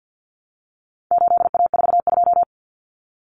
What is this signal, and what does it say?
Morse code, heard as a keyed tone, about 50 wpm, 700 Hz. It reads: 8A42
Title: Common Cut Numbers - 4-Digit 50wpm